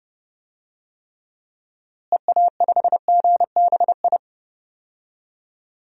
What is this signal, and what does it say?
Morse code, heard as a keyed tone, about 32 wpm, 700 Hz. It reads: EA5GBI
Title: DXCC-100 Call Signs - 6 Characters - Repeated 3x - 30wpm